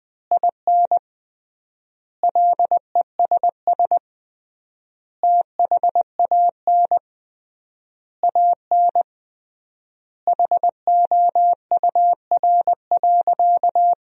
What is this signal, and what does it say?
Morse code, heard as a keyed tone, about 21 wpm, 700 Hz. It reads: IN LESS THAN AN HOUR.
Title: 3x Word-Spacing - Sentences from Top 500 Words - 20wpm